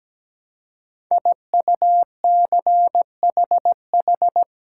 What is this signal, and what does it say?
Morse code, heard as a keyed tone, about 17 wpm, 700 Hz. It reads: IUCHH
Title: Code Groups - 5 Letters - 17wpm